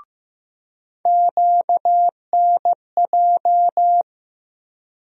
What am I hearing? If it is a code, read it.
Morse code, heard as a keyed tone, about 16 wpm, 700 Hz. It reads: QNJ
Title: USA Call Sign Postfixes - 3 Letters 15wpm